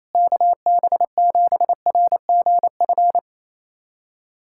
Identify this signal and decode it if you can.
Morse code, heard as a keyed tone, about 28 wpm, 700 Hz. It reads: KB7RGF
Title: USA Callsign DE Callsign 28wpm